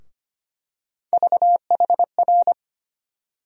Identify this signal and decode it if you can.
Morse code, heard as a keyed tone, about 25 wpm, 700 Hz. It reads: VHR
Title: USA Call Sign Postfixes - 3 Letters 25wpm